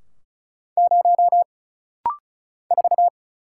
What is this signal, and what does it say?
Morse code, heard as a keyed tone, about 36 wpm, 700 Hz. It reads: 0 4
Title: Single Letter-Number Alt - Rapid-Fire 35wpm